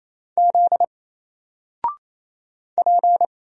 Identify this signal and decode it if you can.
Morse code, heard as a keyed tone, about 28 wpm, 700 Hz. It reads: Z P
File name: Single-Letter-Number-28wpm